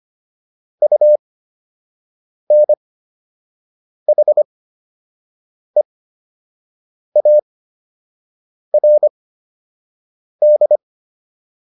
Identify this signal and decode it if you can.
Morse code, heard as a keyed tone, about 25 wpm, spaced as for 6 wpm, 600 Hz. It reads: UNHEARD
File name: UN7 06